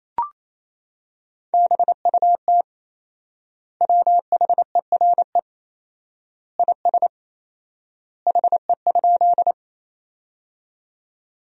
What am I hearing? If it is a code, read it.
Morse code, heard as a keyed tone, about 28 wpm, 700 Hz. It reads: BUT WHERE IS HE?
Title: 4x Word-Spacing - Sentences from Top 200 Words - 28wpm